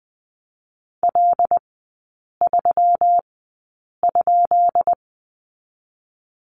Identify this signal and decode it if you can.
Morse code, heard as a keyed tone, about 20 wpm, 700 Hz. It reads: L 3 ?
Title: 133 Lesson 22e - Question Mark - Three Characters 20wpm